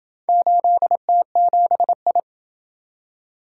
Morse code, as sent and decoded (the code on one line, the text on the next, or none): ---.. - --... ..
8T7I